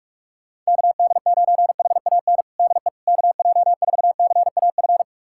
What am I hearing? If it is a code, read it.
Morse code, heard as a keyed tone, about 45 wpm, 690 Hz. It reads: KD9HAN DE KJ4CAF
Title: USA Callsign DE Callsign 45wpm